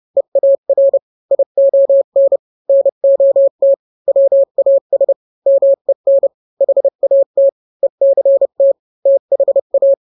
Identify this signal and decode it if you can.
Morse code, heard as a keyed tone, about 30 wpm, 550 Hz. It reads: EAR ION NOT WAS MEN HAT ECT THA